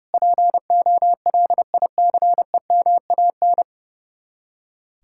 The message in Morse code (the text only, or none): POLICEMAN